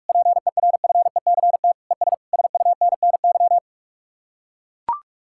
Morse code, heard as a keyed tone, about 44 wpm, 700 Hz. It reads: PERFECT ES SUNNY